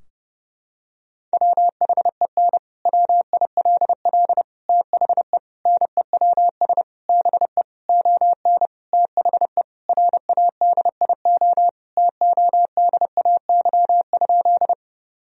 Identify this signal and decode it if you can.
Morse code, heard as a keyed tone, about 32 wpm, 700 Hz. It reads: WHEN WILL THE NEWS BE ON THE RADIO TODAY?
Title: Sentences from Top 1000 Words - Review Part 3 - 30wpm